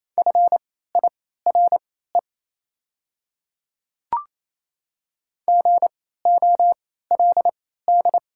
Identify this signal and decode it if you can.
Morse code, heard as a keyed tone, about 28 wpm, spaced as for 16 wpm, 700 Hz. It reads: FIRE GOLD
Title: 3x Character Spacing - Top 500 Words - 28wpm